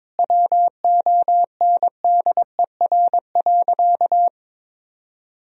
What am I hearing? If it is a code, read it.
Morse code, heard as a keyed tone, about 23 wpm, 700 Hz. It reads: WONDER.
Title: Sentences from Top 1000 Words - Review Part 3 - 22wpm